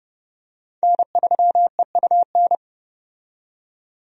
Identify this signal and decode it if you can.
Morse code, heard as a keyed tone, about 30 wpm, 700 Hz. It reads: N3EUN
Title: USA Call Signs - 5 Characters - 30wpm